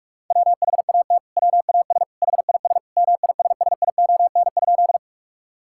Morse code, heard as a keyed tone, about 46 wpm, 700 Hz. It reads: WHAT WAS HIS MISSION?
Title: Sentences from Top 1600 Words 45wpm